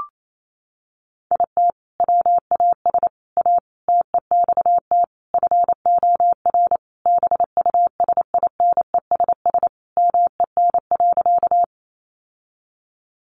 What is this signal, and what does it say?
Morse code, heard as a keyed tone, about 28 wpm, 700 Hz. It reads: IT WAS A TEXT FOR BUSINESS MEN.
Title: Sentences from Top 1200 Words 28wpm